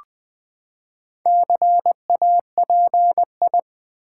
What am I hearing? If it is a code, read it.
Morse code, heard as a keyed tone, about 20 wpm, 700 Hz. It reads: CAPI
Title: Build A Word - 4 Letters At A Time - 10-Letter Words - 20wpm